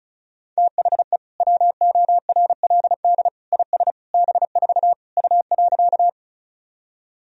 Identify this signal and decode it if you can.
Morse code, heard as a keyed tone, about 34 wpm, 700 Hz. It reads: THE WORLD IS B4 U.
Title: Sentences from Top 200 Words and Abbreviations 35wpm